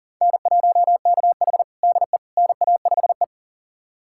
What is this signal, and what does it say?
Morse code, heard as a keyed tone, about 37 wpm, 700 Hz. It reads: N1KH DE NA5E
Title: USA Callsign DE Callsign - 4 Characters - 40wpm